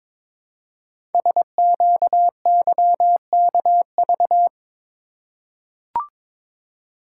Morse code, heard as a keyed tone, about 23 wpm, 700 Hz. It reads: SQYKV
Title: Code Groups - 5 Letters - 22wpm